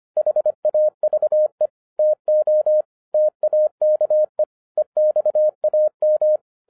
Morse code, heard as a keyed tone, about 25 wpm, 610 Hz. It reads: HAVE TO TAKE EXAM
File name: 2012-01-mp3 / n08-ITS MY FAULT, BUT-25wpm